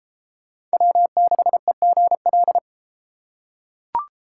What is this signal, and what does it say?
Morse code, heard as a keyed tone, about 32 wpm, 700 Hz. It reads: W6EGL